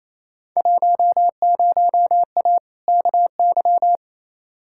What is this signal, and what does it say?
Morse code, heard as a keyed tone, about 28 wpm, 700 Hz. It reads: 10A KY
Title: Field Day Exchanges Encore - 28wpm